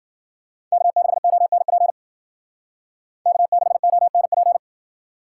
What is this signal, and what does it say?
Morse code, heard as a keyed tone, about 48 wpm, 700 Hz. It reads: K6ONP K6ONP
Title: Speed Racing - USA 1x3 Call Signs 40wpm